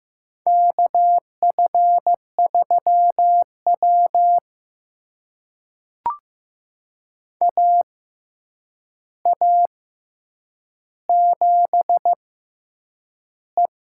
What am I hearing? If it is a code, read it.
Morse code, heard as a keyed tone, about 15 wpm, spaced as for 8 wpm, 700 Hz. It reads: KF3W AA7E
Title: USA Call Signs - 4 Characters - 6x Character Spacing - 15wpm